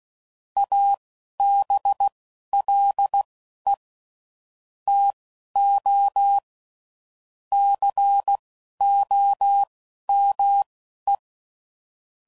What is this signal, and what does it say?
Morse code, heard as a keyed tone, about 16 wpm, spaced as for 11 wpm, 800 Hz. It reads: ABLE TO COME
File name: CW Code Groups @10WPM / QSOs / QSO WA7QOX